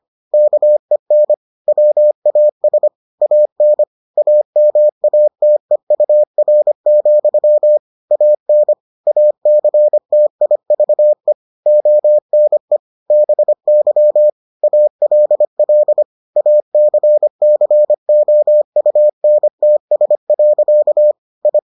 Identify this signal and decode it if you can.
Morse code, heard as a keyed tone, about 25 wpm, 600 Hz. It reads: KEN WAS AN AMATEUR, AN ACTIVE ONE BY ALL ACCOUNTS. I